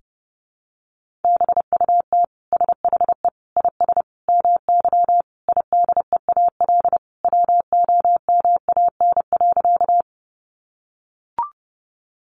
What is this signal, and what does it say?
Morse code, heard as a keyed tone, about 30 wpm, 700 Hz. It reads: BUT SHE IS MY IDEAL WOMAN.
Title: Sentences from Top 1400 Words 30wpm